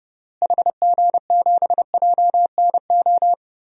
Morse code, heard as a keyed tone, about 30 wpm, 700 Hz. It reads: HG7JNO
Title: DXCC-100 Call Signs - 6 Characters - Repeated 2x - 30wpm